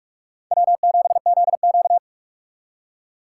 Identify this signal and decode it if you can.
Morse code, heard as a keyed tone, about 47 wpm, 700 Hz. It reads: W7ZQ
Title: USA Call Signs - 4 Characters - 6x Character Spacing - 45wpm